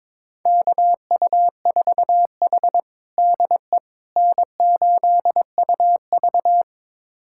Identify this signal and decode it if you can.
Morse code, heard as a keyed tone, about 21 wpm, 700 Hz. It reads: KU4H DE N8UV
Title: USA Callsign DE Callsign - 4 Characters - 22wpm